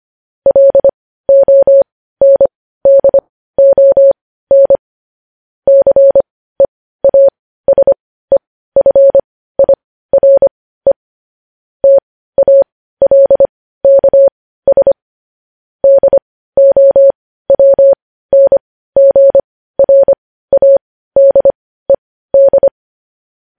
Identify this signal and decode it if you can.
Morse code, heard as a keyed tone, about 25 wpm, spaced as for 15 wpm, 550 Hz. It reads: LONDON CEASEFIRE TALKS DOWNGRADED
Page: news headlines in morse code at 15 wpm